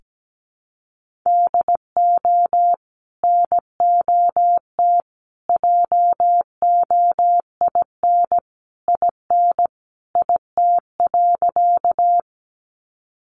Morse code, heard as a keyed tone, about 17 wpm, 700 Hz. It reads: DO NOT JOIN IN IT.
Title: Sentences from Top 1400 Words 17wpm